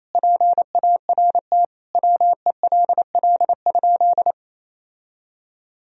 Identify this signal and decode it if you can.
Morse code, heard as a keyed tone, about 28 wpm, 700 Hz. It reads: PART WELL?
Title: Sentences from Top 1000 Words - Review Part 1 - 28wpm